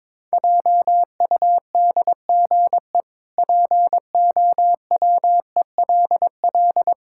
Morse code, heard as a keyed tone, about 22 wpm, 700 Hz. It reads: JUDGE POWELL
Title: Pangram Challenge v2 - 22wpm